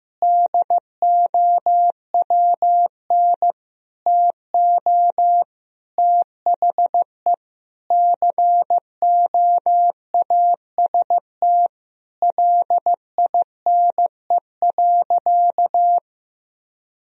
Morse code, heard as a keyed tone, about 16 wpm, 700 Hz. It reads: DOWN TO THE COAST LINE.
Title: Sentences from Top 1000 Words - Review Part 4 - 15wpm